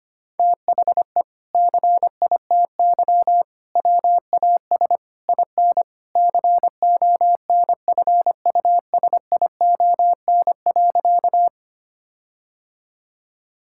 Morse code, heard as a keyed tone, about 25 wpm, 700 Hz. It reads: THE CITY WAS IN CONFUSION.
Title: Sentences from Top 1600 Words 25wpm